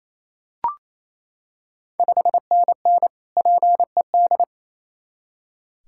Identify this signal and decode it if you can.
Morse code, heard as a keyed tone, about 29 wpm, 690 Hz. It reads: 5NN PED
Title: Ontario QSO Party Exchange 28wpm